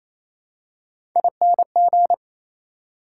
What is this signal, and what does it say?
Morse code, heard as a keyed tone, about 28 wpm, 700 Hz. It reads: ING
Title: Prefixes and Suffixes 28wpm